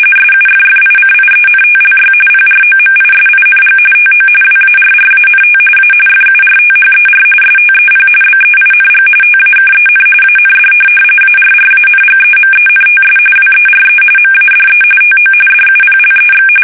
RTTY, about 45.45 baud. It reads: SUFFICIENTLY PROMISING TO WARRANT A FURTHER STEP BEING TAKEN AND ARRANGEMENTS WERE MADE TO DUPLEX